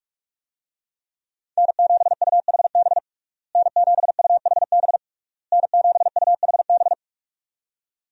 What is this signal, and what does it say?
Morse code, heard as a keyed tone, about 46 wpm, 700 Hz. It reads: N7UHB N7UHB N7UHB
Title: USA Call Signs - 5 Characters - Repeated 3x - 45wpm